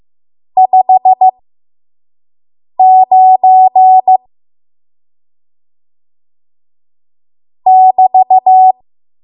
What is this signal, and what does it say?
Morse code, heard as a keyed tone, about 15 wpm, spaced as for 5 wpm, 750 Hz. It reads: 59 <BT>
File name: Archive / 5 WPM / 05WPM